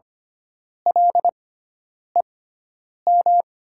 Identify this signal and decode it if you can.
Morse code, heard as a keyed tone, about 25 wpm, spaced as for 9 wpm, 700 Hz. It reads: LEM
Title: 6x Character Spacing - Top 500 Words - 25wpm